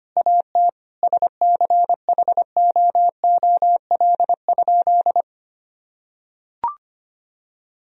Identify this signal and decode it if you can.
Morse code, heard as a keyed tone, about 25 wpm, 700 Hz. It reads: AT SCHOOL?
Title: Sentences from Top 1000 Words - Review Part 3 - 25wpm